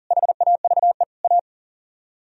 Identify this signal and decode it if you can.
Morse code, heard as a keyed tone, about 52 wpm, 700 Hz. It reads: HAVE A